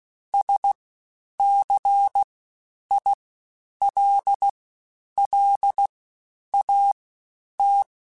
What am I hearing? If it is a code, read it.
Morse code, heard as a keyed tone, about 16 wpm, spaced as for 9 wpm, 800 Hz. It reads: SCILLAT
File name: Special INGs-mp3s / Special INGs-mp3s / Fast ing @15